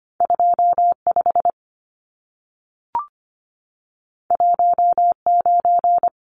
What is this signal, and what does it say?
Morse code, heard as a keyed tone, about 25 wpm, 700 Hz. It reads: 25 19
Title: Numbers - 2-Digit 25wpm